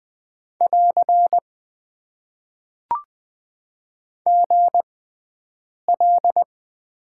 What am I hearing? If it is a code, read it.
Morse code, heard as a keyed tone, about 20 wpm, spaced as for 7 wpm, 700 Hz. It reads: <AR> GL
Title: Top 90 QSO Elements - 6x Character Spacing - 20wpm